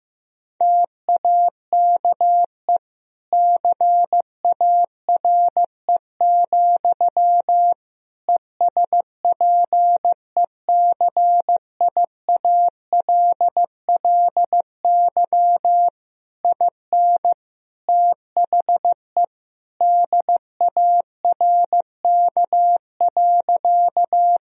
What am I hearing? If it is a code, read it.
Morse code, heard as a keyed tone, about 15 wpm, 700 Hz. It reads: TAKE CARE, ESPECIALLY IN THE DARK.
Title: Sentences from Top 1000 Words - Review Part 4 - 15wpm